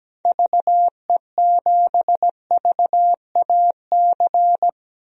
Morse code, heard as a keyed tone, about 17 wpm, 700 Hz. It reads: VE7VAC
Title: DXCC-100 Call Signs - 6 Characters - 17wpm